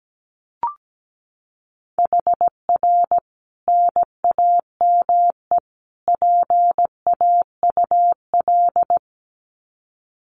Morse code, heard as a keyed tone, about 17 wpm, 700 Hz. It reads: HR NAME PAUL
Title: Common QSO Phrases 17wpm